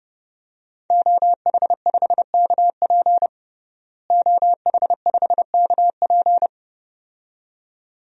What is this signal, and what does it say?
Morse code, heard as a keyed tone, about 30 wpm, 700 Hz. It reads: OH5KP OH5KP
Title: DXCC-100 Call Signs - 5 Characters - Repeated 2x - 30wpm